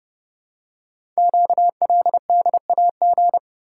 Morse code, heard as a keyed tone, about 30 wpm, 700 Hz. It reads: QLDAG
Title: Code Groups - 5 Letters - 30wpm